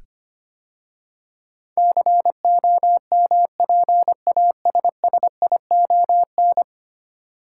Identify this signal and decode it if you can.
Morse code, heard as a keyed tone, about 25 wpm, 700 Hz. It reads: COMPASSION
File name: 10 Letter Words - First 3 Letters Given - 25wpm